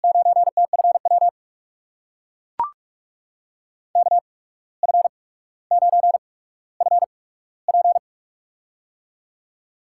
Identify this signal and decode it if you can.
Morse code, heard as a keyed tone, about 42 wpm, spaced as for 18 wpm, 700 Hz. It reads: 9TFW KF9FP
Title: USA Call Signs - 5 Characters - 8x Character Spacing - 45wpm